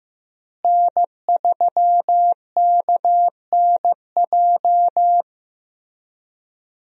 Morse code, heard as a keyed tone, about 15 wpm, 700 Hz. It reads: N3KNJ
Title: USA Call Signs - 5 Characters - Repeated 2x - 15wpm